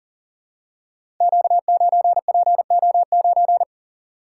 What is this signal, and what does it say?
Morse code, heard as a keyed tone, about 40 wpm, 700 Hz. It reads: Q9PO9
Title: Code Groups - 5 Letter:Numbers - 40wpm